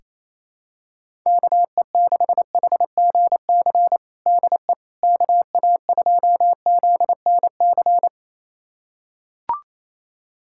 Morse code, heard as a keyed tone, about 28 wpm, 700 Hz. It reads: KE6HGC DE KA2ZNC
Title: USA Callsign DE Callsign - 6 Characters - 28wpm